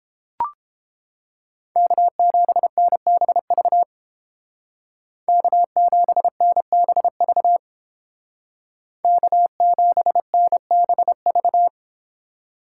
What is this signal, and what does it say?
Morse code, heard as a keyed tone, about 29 wpm, 700 Hz. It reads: K7NBV K7NBV K7NBV